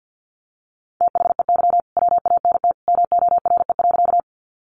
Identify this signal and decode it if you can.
Morse code, heard as a keyed tone, about 50 wpm, 700 Hz. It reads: THEY WANT MORE.
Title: Sentences from Top 200 Words 50wpm